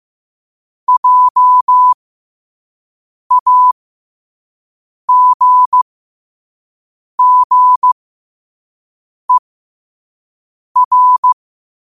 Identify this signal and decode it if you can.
Morse code, heard as a keyed tone, about 15 wpm, spaced as for 5 wpm, 1000 Hz. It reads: JAGGER
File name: qotdfiles / 5wpm